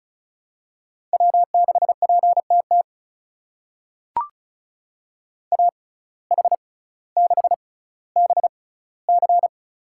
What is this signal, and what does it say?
Morse code, heard as a keyed tone, about 36 wpm, spaced as for 18 wpm, 700 Hz. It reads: W6PTT AH6BC
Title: USA Call Signs - 5 Characters - 6x Character Spacing - 35wpm